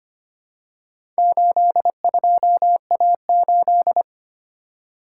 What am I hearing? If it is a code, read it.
Morse code, heard as a keyed tone, about 25 wpm, 690 Hz. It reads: 82A8
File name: practice / 25wpm / Four Letter-Number Combinations - 25wpm